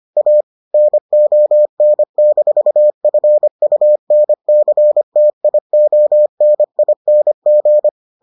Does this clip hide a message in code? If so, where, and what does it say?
Yes: Morse code from 0.2 to 7.9 s, heard as a keyed tone, about 25 wpm, 600 Hz. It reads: A NON-FUNCTIONING